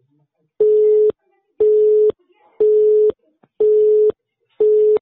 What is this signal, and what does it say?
Busy tone (Europe, 425 Hz).